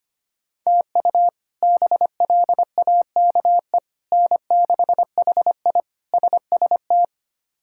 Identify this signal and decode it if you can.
Morse code, heard as a keyed tone, about 25 wpm, 700 Hz. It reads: TU BLAKE N6HI SST
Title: K1USN SST QSO Practice - 25wpm